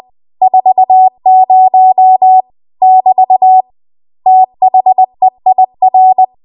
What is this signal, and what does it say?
Morse code, heard as a keyed tone, about 20 wpm, 750 Hz. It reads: 40 <BT> THEIR
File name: Archive / 20 WPM / 20WPM